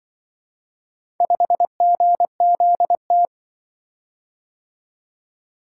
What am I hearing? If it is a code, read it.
Morse code, heard as a keyed tone, about 23 wpm, 700 Hz. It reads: 5GZT
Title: Speed-Racing - Four Letter-Number Combinations - 20wpm